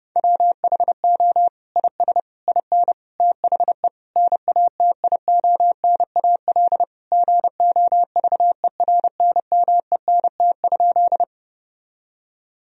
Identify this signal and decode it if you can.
Morse code, heard as a keyed tone, about 31 wpm, 700 Hz. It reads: WHO IS IN THE NATIONAL GOVERNMENT?
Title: Sentences from Top 1000 Words - Review Part 4 - 30wpm